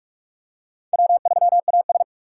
Morse code, heard as a keyed tone, about 44 wpm, 700 Hz. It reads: W3AS
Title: USA Call Signs - 4 Characters - 2x Character Spacing - 45wpm